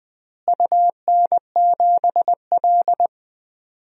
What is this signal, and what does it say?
Morse code, heard as a keyed tone, about 20 wpm, 700 Hz. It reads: UN7L
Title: DXCC-100 Call Signs - 4 Characters - Repeated 2x - 20wpm